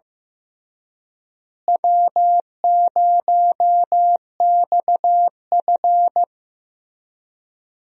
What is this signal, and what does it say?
Morse code, heard as a keyed tone, about 16 wpm, 700 Hz. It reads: W0XF
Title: USA Call Signs - 4 Characters - Repeated 3x - 15wpm